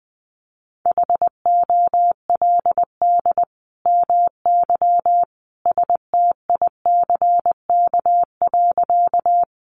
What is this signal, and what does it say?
Morse code, heard as a keyed tone, about 20 wpm, 700 Hz. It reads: HOLD MY STICK.
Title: Sentences from Top 1200 Words 20wpm